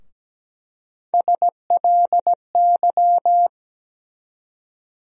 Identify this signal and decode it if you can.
Morse code, heard as a keyed tone, about 17 wpm, 700 Hz. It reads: SLY
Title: USA Call Sign Postfixes - 3 Letters 17wpm